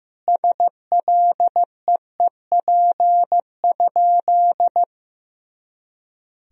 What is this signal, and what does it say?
Morse code, heard as a keyed tone, about 15 wpm, 700 Hz. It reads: SLEEP?
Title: Sentences from Top 500 Words and Abbreviations 15wpm